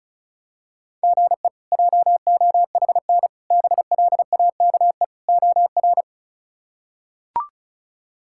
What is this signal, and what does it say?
Morse code, heard as a keyed tone, about 34 wpm, 700 Hz. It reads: GE JOHN BLAKE OR
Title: K1USN SST QSO Practice - 35wpm